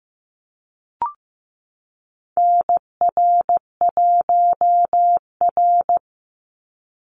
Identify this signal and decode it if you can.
Morse code, heard as a keyed tone, about 14 wpm, 700 Hz. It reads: NR1R
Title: Call Signs Repeated 3x - 15wpm